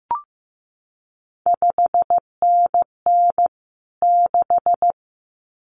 Morse code, heard as a keyed tone, about 15 wpm, 700 Hz. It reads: 5NN 6
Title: CQ WW DX Contest Exchange Encore - 15wpm